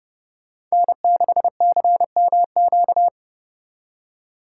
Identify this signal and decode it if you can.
Morse code, heard as a keyed tone, about 35 wpm, 700 Hz. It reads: N6CMQ
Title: Speed Racing - USA 1x3 Call Signs 30wpm